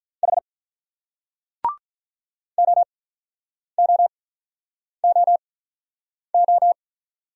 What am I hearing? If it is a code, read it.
Morse code, heard as a keyed tone, about 47 wpm, spaced as for 9 wpm, 700 Hz. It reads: H OOOO